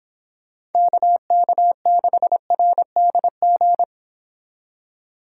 Morse code, heard as a keyed tone, about 29 wpm, 700 Hz. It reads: KK6RDG